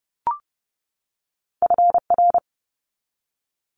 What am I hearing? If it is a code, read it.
Morse code, heard as a keyed tone, about 30 wpm, 700 Hz. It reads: FR
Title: Multi-letter Phonemes 30wpm